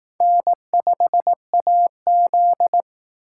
Morse code, heard as a keyed tone, about 18 wpm, 700 Hz. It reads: N5AZ
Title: Speed Racing - USA 1x2 Call Signs 15wpm